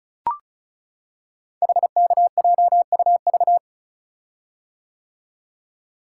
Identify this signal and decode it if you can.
Morse code, heard as a keyed tone, about 36 wpm, 690 Hz. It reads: HKJUV
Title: Code Groups - 5 Letters - 35wpm